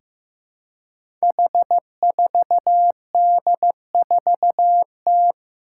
Morse code, heard as a keyed tone, about 15 wpm, 700 Hz. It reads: H4D4T
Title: Five Challenging Letter-Number Combinations 15wpm